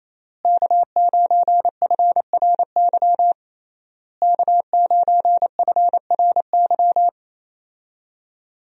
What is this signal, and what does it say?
Morse code, heard as a keyed tone, about 27 wpm, 700 Hz. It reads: K9FRY K9FRY
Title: USA Call Signs - 5 Characters - Repeated 3x - 28wpm